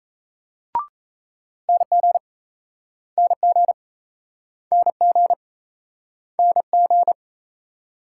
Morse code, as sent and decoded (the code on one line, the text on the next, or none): -. --. / -. --. / -. --. / -. --.
NG NG NG NG